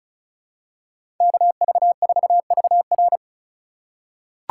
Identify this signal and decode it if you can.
Morse code, heard as a keyed tone, about 34 wpm, 700 Hz. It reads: KV4VR
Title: USA Call Signs - 4x Character Spacing - 35wpm